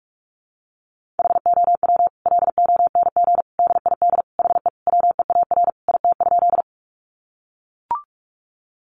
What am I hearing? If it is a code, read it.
Morse code, heard as a keyed tone, about 44 wpm, 700 Hz. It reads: HOW LONG DID HE WEAR IT?
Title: Sentences from Top 1500 Words 45wpm